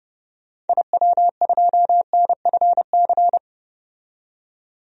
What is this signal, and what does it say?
Morse code, heard as a keyed tone, about 30 wpm, 700 Hz. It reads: IW2NFC
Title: DXCC-100 Call Signs - 6 Characters - 30wpm